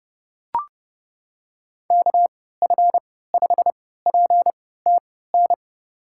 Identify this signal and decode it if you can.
Morse code, heard as a keyed tone, about 30 wpm, spaced as for 17 wpm, 700 Hz. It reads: KF5PTN
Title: USA Call Signs - 6 Characters - 3x Character Spacing - 30wpm